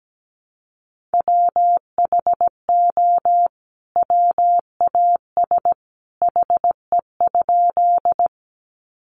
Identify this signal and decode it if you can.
Morse code, heard as a keyed tone, about 17 wpm, 700 Hz. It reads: WHO WAS HE?
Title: Sentences from Top 100 Words 17wpm